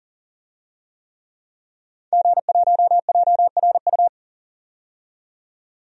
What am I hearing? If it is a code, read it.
Morse code, heard as a keyed tone, about 40 wpm, 700 Hz. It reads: G1JRU
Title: DXCC-100 Call Signs - 5 Characters - 40wpm